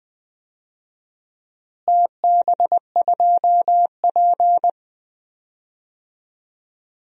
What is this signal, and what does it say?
Morse code, heard as a keyed tone, about 20 wpm, 700 Hz. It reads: TB2P
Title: DXCC-100 Call Signs - 4 Characters - Repeated 2x - 20wpm